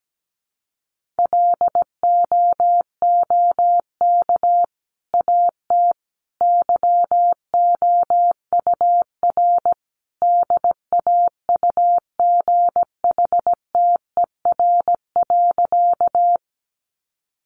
Morse code, heard as a keyed tone, about 17 wpm, 700 Hz. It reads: LOOK AT YOUR DAUGHTER.
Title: Sentences from Top 1500 Words 17wpm